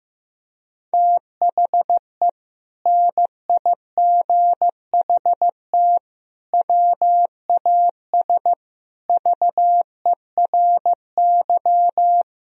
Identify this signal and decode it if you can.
Morse code, heard as a keyed tone, about 15 wpm, 700 Hz. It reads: THE NIGHT WAS VERY
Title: Sentences from Top 1000 Words - Review Part 2 - 15wpm